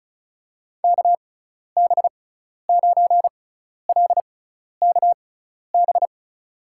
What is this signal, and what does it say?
Morse code, heard as a keyed tone, about 34 wpm, spaced as for 12 wpm, 700 Hz. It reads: KB9LKB
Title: USA Call Signs - 6x Character Spacing - 35wpm